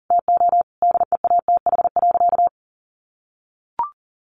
Morse code, heard as a keyed tone, about 40 wpm, 700 Hz. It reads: TO DEATH.
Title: Sentences from Top 1100 Words 40wpm